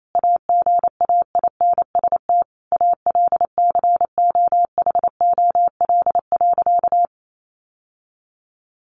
Morse code, heard as a keyed tone, about 28 wpm, 700 Hz. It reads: AGAINST ALCOHOL.
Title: Sentences from Top 1400 Words 28wpm